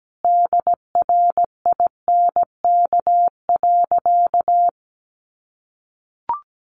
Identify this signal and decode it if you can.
Morse code, heard as a keyed tone, about 17 wpm, 700 Hz. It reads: DRINK.
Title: Sentences from Top 1500 Words 17wpm